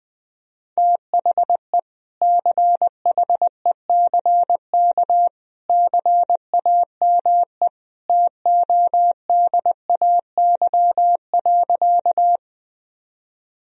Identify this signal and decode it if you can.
Morse code, heard as a keyed tone, about 20 wpm, 700 Hz. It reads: THE CHECK CAME TODAY.
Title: Sentences from Top 1000 Words - Review Part 3 - 20wpm